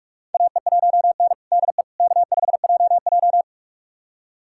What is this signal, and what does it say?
Morse code, heard as a keyed tone, about 46 wpm, 700 Hz. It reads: AE1N DE K5JJ
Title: USA Callsign DE Callsign - 4 Characters - 45wpm